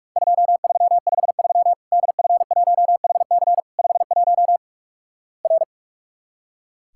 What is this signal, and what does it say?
Morse code, heard as a keyed tone, about 44 wpm, 700 Hz. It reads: 2353 DF1HC 51 R
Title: Worked All Europe DX Contest - QTC Exchange - 45wpm